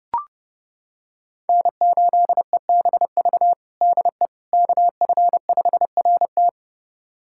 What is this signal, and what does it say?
Morse code, heard as a keyed tone, about 30 wpm, 700 Hz. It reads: N8EBV DE KF5RT
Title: USA Callsign DE Callsign - 5 Characters - 30wpm